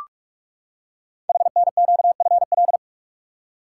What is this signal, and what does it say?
Morse code, heard as a keyed tone, about 44 wpm, 700 Hz. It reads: HNQFL
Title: Code Groups - 5 Letters - 45wpm